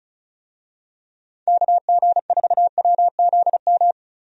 Morse code, heard as a keyed tone, about 36 wpm, 700 Hz. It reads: KG4WZM
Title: USA Call Signs - 6 Characters - 35wpm